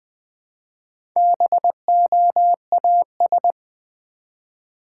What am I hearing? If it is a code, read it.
Morse code, heard as a keyed tone, about 20 wpm, 700 Hz. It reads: BOAS
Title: Build A Word - 1 Letter At A Time - 5-Letter Words - 20wpm